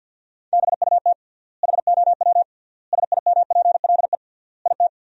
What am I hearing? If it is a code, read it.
Morse code, heard as a keyed tone, about 47 wpm, 700 Hz. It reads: BUT HOW SIMPLE IT
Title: 3x Word-Spacing - Sentences from Top 400 Words - 50wpm